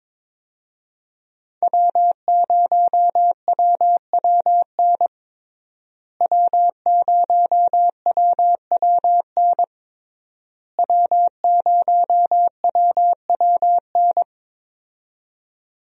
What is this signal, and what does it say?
Morse code, heard as a keyed tone, about 22 wpm, 700 Hz. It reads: W0WWN W0WWN W0WWN
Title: USA Call Signs - 5 Characters - Repeated 3x - 22wpm